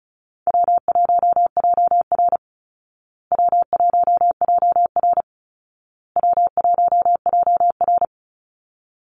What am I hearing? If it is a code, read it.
Morse code, heard as a keyed tone, about 34 wpm, 700 Hz. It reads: W1JR W1JR W1JR
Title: Call Signs Repeated 2x - 35wpm